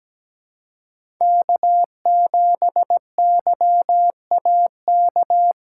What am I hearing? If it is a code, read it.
Morse code, heard as a keyed tone, about 17 wpm, 700 Hz. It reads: K7YAK
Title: USA Callsign DE Callsign - 5 Characters - 17wpm